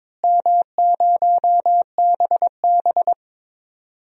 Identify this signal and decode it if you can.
Morse code, heard as a keyed tone, about 22 wpm, 700 Hz. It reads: M0BB
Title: DXCC-100 Call Signs - 4 Characters - Repeated 3x - 22wpm